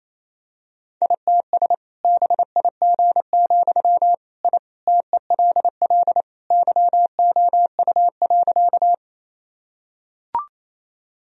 Morse code, heard as a keyed tone, about 28 wpm, 700 Hz. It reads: ITS BIG, I TELL YOU.
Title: Sentences from Top 1000 Words - Review Part 1 - 28wpm